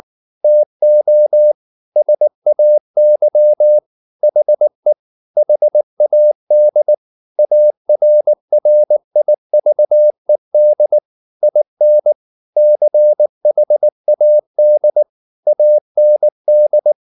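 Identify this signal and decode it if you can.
Morse code, heard as a keyed tone, about 19 wpm, 600 Hz. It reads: TO SAY HE HAD ARRIVED IN CHAD AND
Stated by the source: Morse code at 19 wpm from CW med Gnister, reading Ken Follett's novel Never in English.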